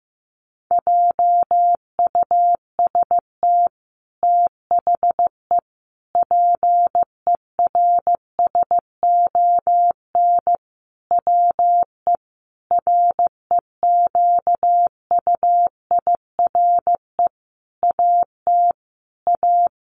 Morse code, heard as a keyed tone, about 15 wpm, 700 Hz. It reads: JUST THE PERSON WE REQUIRE AT A
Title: Sentences from Top 900 Words 15wpm